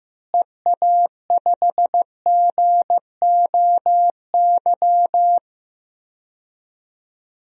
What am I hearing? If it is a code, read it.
Morse code, heard as a keyed tone, about 15 wpm, 700 Hz. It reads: EA5GOY
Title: DXCC-100 Call Signs - 6 Characters - Repeated 3x - 15wpm